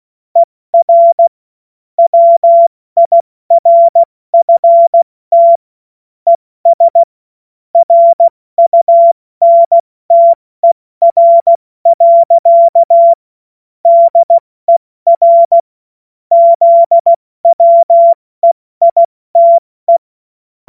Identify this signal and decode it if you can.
Morse code, heard as a keyed tone, about 16 wpm, 680 Hz. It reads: ER WIRFT ES RUNTER. DER ZWEITE
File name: Morsekurs / Witze / Witz 01. November 2022 70 F80